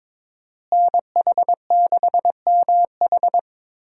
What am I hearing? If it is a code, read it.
Morse code, heard as a keyed tone, about 22 wpm, 700 Hz. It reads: NH6MH
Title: USA Call Signs - 5 Characters - Repeated 2x - 22wpm